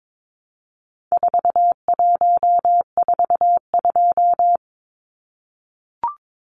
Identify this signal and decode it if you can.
Morse code, heard as a keyed tone, about 22 wpm, 700 Hz. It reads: 4142
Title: Numbers - 4-Digit 22wpm